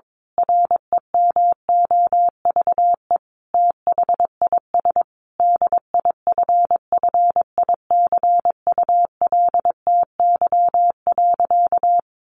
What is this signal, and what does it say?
Morse code, heard as a keyed tone, about 22 wpm, 700 Hz. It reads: REMOVE THIS DIFFICULTY.
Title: Sentences from Top 1500 Words 22wpm